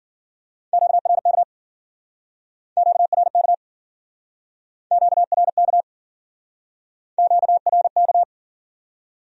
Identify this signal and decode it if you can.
Morse code, heard as a keyed tone, about 54 wpm, 700 Hz. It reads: QRK QRK QRK QRK